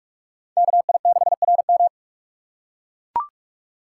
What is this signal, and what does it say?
Morse code, heard as a keyed tone, about 41 wpm, 700 Hz. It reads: KI6RM